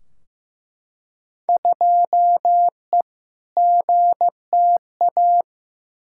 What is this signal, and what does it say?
Morse code, heard as a keyed tone, about 15 wpm, 700 Hz. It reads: 2E GTA
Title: Field Day Exchanges Encore - 15wpm